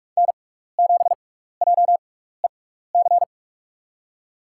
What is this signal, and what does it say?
Morse code, heard as a keyed tone, about 42 wpm, 700 Hz. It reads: N 7 J E C
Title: USA Call Signs - 5 Characters - 6x Character Spacing - 45wpm